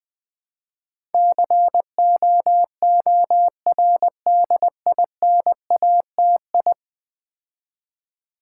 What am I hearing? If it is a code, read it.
Morse code, heard as a keyed tone, about 20 wpm, 700 Hz. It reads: COORDINATI
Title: Build A Word - 5 Letters At A Time - 12-Letter Words - 20wpm